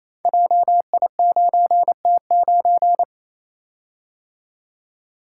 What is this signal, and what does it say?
Morse code, heard as a keyed tone, about 28 wpm, 690 Hz. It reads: JI9T9
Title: Code Groups - 5 Letter:Numbers - 28wpm